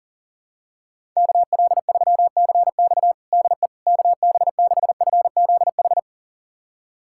Morse code, heard as a keyed tone, about 40 wpm, 700 Hz. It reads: KL3CX DE KB6FZH
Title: USA Callsign DE Callsign 40wpm